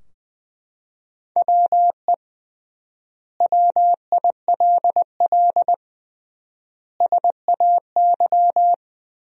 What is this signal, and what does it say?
Morse code, heard as a keyed tone, about 20 wpm, 700 Hz. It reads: WE WILL SAY
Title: 3x Word-Spacing - Sentences from Top 200 Words - 20wpm